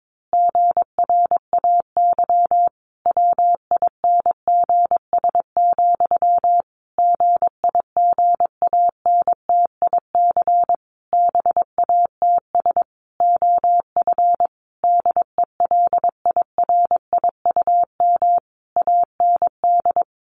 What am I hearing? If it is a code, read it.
Morse code, heard as a keyed tone, about 22 wpm, 700 Hz. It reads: GRAY WINGS, GIGANTIC BATS OF DELIRIUM AND
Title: A Sound Of Thunder 22wpm